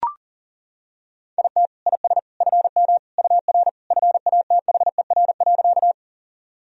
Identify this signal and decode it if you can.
Morse code, heard as a keyed tone, about 40 wpm, 700 Hz. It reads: IT IS FM UR FATHER.
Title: Sentences from Top 300 Words and Abbreviations 40wpm